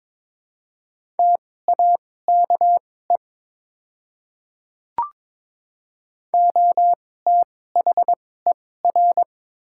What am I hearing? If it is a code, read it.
Morse code, heard as a keyed tone, about 22 wpm, spaced as for 16 wpm, 700 Hz. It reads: TAKE OTHER
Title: 2x Character Spacing - Top 200 Words - 22wpm